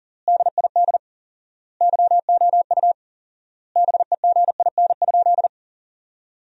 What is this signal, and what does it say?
Morse code, heard as a keyed tone, about 37 wpm, 700 Hz. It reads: DID YOU BEGIN?
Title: 4x Word-Spacing - Sentences from Top 300 Words - 40wpm